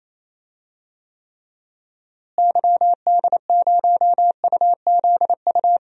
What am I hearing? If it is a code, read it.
Morse code, heard as a keyed tone, about 28 wpm, 700 Hz. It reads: YD0UZU
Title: DXCC-100 Call Signs - 6 Characters - Repeated 3x - 28wpm